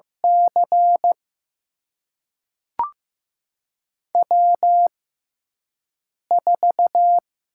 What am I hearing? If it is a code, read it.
Morse code, heard as a keyed tone, about 15 wpm, spaced as for 5 wpm, 700 Hz. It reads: C W4
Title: USA Call Signs - 5 Characters - 6x Character Spacing - 15wpm